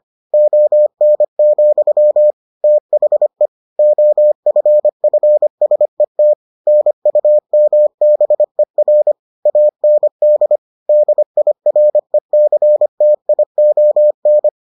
Morse code, heard as a keyed tone, about 25 wpm, 600 Hz. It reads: ON, THE OFFSET NUMBER AND DIRECTION